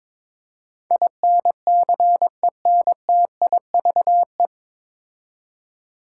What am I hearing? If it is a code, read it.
Morse code, heard as a keyed tone, about 22 wpm, 700 Hz. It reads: INCENTIVE
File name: 9 Letter Words - First 3 Letters Given - 22wpm